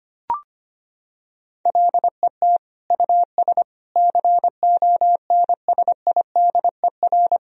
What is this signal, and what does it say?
Morse code, heard as a keyed tone, about 25 wpm, 700 Hz. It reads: LET US CONSIDER
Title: Sentences from Top 1000 Words - Review Part 4 - 25wpm